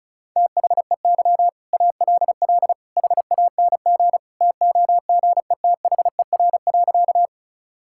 Morse code, heard as a keyed tone, about 36 wpm, 700 Hz. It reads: THEY ALL HANG TOGETHER.
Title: Sentences from Top 1600 Words 35wpm